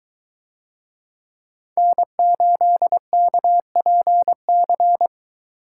Morse code, heard as a keyed tone, about 23 wpm, 700 Hz. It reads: N8KPC